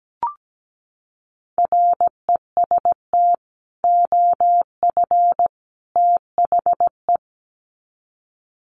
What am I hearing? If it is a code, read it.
Morse code, heard as a keyed tone, about 17 wpm, 700 Hz. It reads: REST OF THE
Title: Sets of 3 Words 17wpm